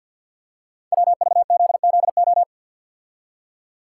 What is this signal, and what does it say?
Morse code, heard as a keyed tone, about 51 wpm, 700 Hz. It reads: WVZZO
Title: Code Groups - 5 Letters - 50wpm